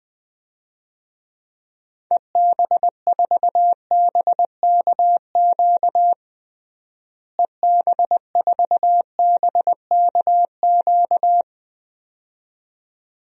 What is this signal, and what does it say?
Morse code, heard as a keyed tone, about 20 wpm, 700 Hz. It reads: EB4BKQ EB4BKQ
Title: DXCC-100 Call Signs - 6 Characters - Repeated 2x - 20wpm